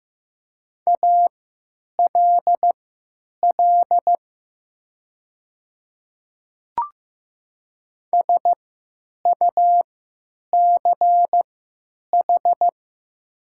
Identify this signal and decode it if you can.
Morse code, heard as a keyed tone, about 15 wpm, spaced as for 9 wpm, 700 Hz. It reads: ALL SUCH
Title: 3x Character Spacing - Top 200 Words - 15wpm